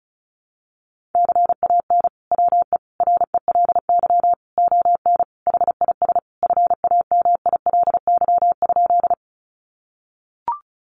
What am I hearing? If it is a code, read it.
Morse code, heard as a keyed tone, about 34 wpm, 700 Hz. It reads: CAN WE RELY ON HIS FAMILY?
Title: Sentences from Top 1200 Words 35wpm